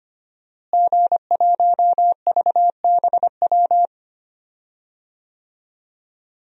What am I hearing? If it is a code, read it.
Morse code, heard as a keyed tone, about 25 wpm, 700 Hz. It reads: G1VBW
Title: DXCC-100 Call Signs - 5 Characters - Repeated 2x - 25wpm